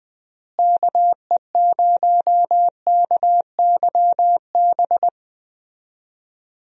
Morse code, heard as a keyed tone, about 20 wpm, 700 Hz. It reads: KE0KYB
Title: USA Call Signs - 6 Characters - Repeated 3x - 20wpm